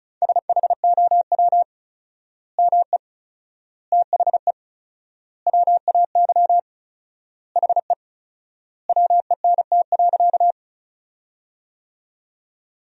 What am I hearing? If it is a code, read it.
Morse code, heard as a keyed tone, about 36 wpm, 700 Hz. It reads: SHOW ME THE WAY HE WENT.
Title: 4x Word-Spacing - Sentences from Top 200 Words - 35wpm